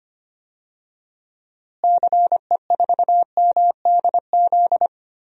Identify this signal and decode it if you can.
Morse code, heard as a keyed tone, about 25 wpm, 700 Hz. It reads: CE4MDZ
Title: DXCC-100 Call Signs - 6 Characters - Repeated 3x - 25wpm